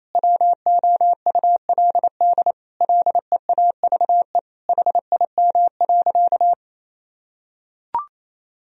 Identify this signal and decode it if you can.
Morse code, heard as a keyed tone, about 28 wpm, 700 Hz. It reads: WOULD LEAVE HIM.
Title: Sentences from Top 1000 Words - Review Part 4 - 28wpm